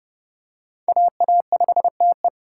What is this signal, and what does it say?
Morse code, heard as a keyed tone, about 28 wpm, 700 Hz. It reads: AA5TE